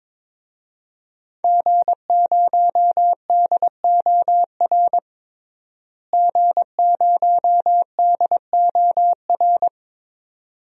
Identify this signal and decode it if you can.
Morse code, heard as a keyed tone, about 22 wpm, 700 Hz. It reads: G0DOR G0DOR
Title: DXCC-100 Call Signs - 5 Characters - Repeated 2x - 22wpm